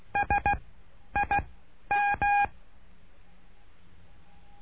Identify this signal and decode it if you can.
Morse code, heard as a keyed tone, about 16 wpm, 810 Hz. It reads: S I M
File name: cursos / CURSO DE TELEGRAFIA / Telegrafia - Trilha 09